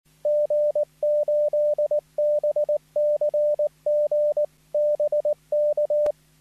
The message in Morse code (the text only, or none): G8BCG BK